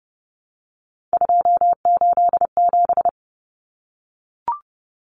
Morse code, heard as a keyed tone, about 29 wpm, 700 Hz. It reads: 287
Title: Numbers - 3-Digit 30wpm